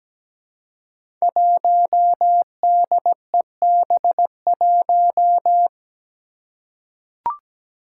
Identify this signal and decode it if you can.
Morse code, heard as a keyed tone, about 17 wpm, 700 Hz. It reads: 1DEB1
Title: Five Challenging Letter-Number Combinations 17wpm